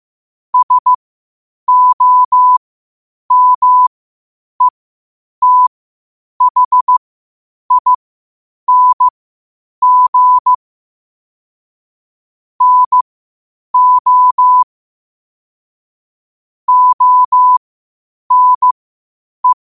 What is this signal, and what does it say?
Morse code, heard as a keyed tone, about 15 wpm, spaced as for 8 wpm, 1000 Hz. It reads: SOMETHING NO ONE
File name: qotdfiles / 8wpm